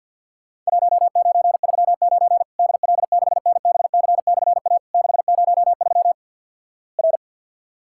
Morse code, heard as a keyed tone, about 48 wpm, 700 Hz. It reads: 1939 DL6NBC/A 603 R
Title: Worked All Europe DX Contest - QTC Exchange - 50wpm